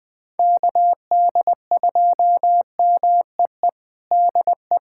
Morse code, heard as a keyed tone, about 21 wpm, 700 Hz. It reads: KD2MEE DE
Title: USA Callsign DE Callsign - 6 Characters - 20wpm